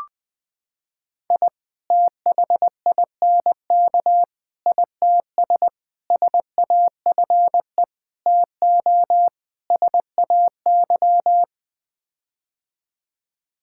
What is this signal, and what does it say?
Morse code, heard as a keyed tone, about 21 wpm, 700 Hz. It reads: I THINK ITS SAFE TO SAY
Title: Sets of 6 Words - 20wpm